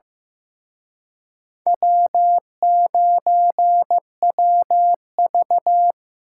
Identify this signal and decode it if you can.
Morse code, heard as a keyed tone, about 18 wpm, 700 Hz. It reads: W9WV
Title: Speed Racing - USA 1x2 Call Signs 15wpm